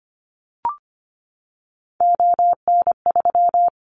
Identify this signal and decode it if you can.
Morse code, heard as a keyed tone, about 25 wpm, 700 Hz. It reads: ON3
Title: Callsign Prefixes for Top 25 DXCC - 25wpm